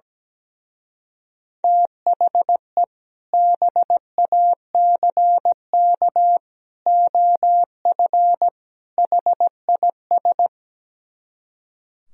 Morse code, heard as a keyed tone, about 17 wpm, 700 Hz. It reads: THE BACK OF HIS
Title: Speed-Racing - Sets of 4 Words 17wpm